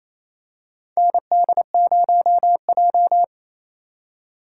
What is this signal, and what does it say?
Morse code, heard as a keyed tone, about 27 wpm, 700 Hz. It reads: ND0J
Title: USA Call Signs - 4 Characters - 4x Character Spacing - 28wpm